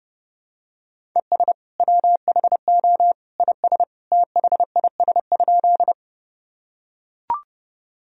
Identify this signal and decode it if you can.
Morse code, heard as a keyed tone, about 32 wpm, 700 Hz. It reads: ES WHO IS THIS?
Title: Sentences from Top 100 Words and Abbreviations 30wpm